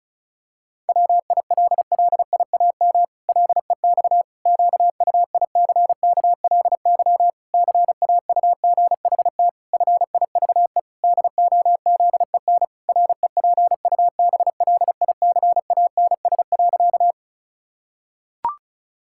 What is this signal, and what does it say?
Morse code, heard as a keyed tone, about 36 wpm, 700 Hz. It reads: WILLIAM LEX QUICKLY CAUGHT FIVE DOZEN REPUBLICANS.
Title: Pangram Challenge v2 - 35wpm